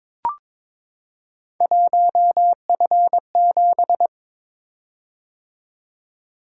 Morse code, heard as a keyed tone, about 22 wpm, 700 Hz. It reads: 1F7
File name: practice / 22wpm / Three Letter-Number Combinations - 22wpm